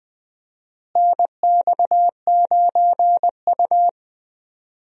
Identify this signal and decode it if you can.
Morse code, heard as a keyed tone, about 20 wpm, 700 Hz. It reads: NX9U
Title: USA Call Signs - 4 Characters - Repeated 3x - 20wpm